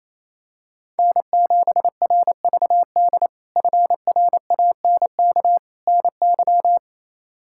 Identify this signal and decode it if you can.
Morse code, heard as a keyed tone, about 28 wpm, 700 Hz. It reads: N7RVD FRANK NY
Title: K1USN SST QSO Practice - 28wpm